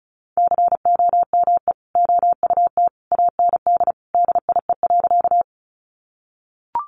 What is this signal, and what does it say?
Morse code, heard as a keyed tone, about 34 wpm, 700 Hz. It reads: COME OUT AND DIE.
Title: Sentences from Top 1300 Words 35wpm